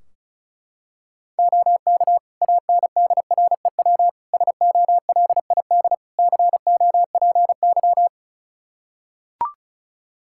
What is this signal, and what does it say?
Morse code, heard as a keyed tone, about 36 wpm, 700 Hz. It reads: OK ANDREW SOLID COPY
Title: Practice QSO - Exchange 3 - More Info - 35wpm